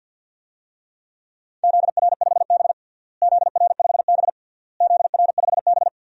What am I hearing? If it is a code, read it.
Morse code, heard as a keyed tone, about 50 wpm, 700 Hz. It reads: ZR5B ZR5B ZR5B
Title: DXCC-100 Call Signs - 4 Characters - Repeated 3x - 50wpm